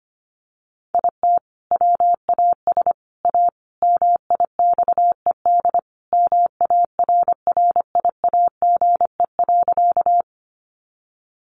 Morse code, heard as a keyed tone, about 25 wpm, 700 Hz. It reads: IT WAS A MIXED MARRIAGE.
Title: Sentences from Top 1500 Words 25wpm